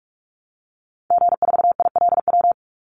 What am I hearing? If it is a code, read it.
Morse code, heard as a keyed tone, about 44 wpm, 700 Hz. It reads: G4ILW
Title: Call Signs Repeated 2x - 45wpm